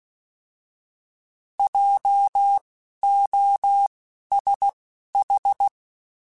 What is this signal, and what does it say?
Morse code, heard as a keyed tone, about 16 wpm, spaced as for 11 wpm, 800 Hz. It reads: JOSH
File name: Alphabetical / Alphabetical mp3s / Js @10